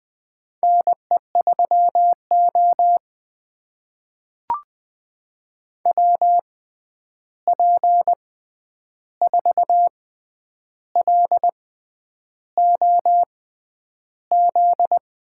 Morse code, heard as a keyed tone, about 20 wpm, spaced as for 9 wpm, 700 Hz. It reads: NE3O WP4LOZ
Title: USA Call Signs - 6x Character Spacing - 20wpm